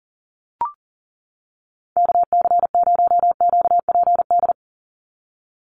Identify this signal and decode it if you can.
Morse code, heard as a keyed tone, about 40 wpm, 700 Hz. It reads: KC0QPD
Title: USA Call Signs 40wpm